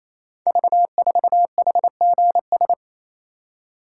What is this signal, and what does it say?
Morse code, heard as a keyed tone, about 29 wpm, 700 Hz. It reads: V4HGS
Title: Code Groups - 5 Letter:Numbers - 28wpm